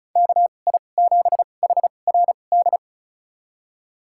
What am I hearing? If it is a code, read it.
Morse code, heard as a keyed tone, about 34 wpm, spaced as for 25 wpm, 700 Hz. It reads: KI7HRD
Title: USA Call Signs - 6 Characters - 2x Character Spacing - 35wpm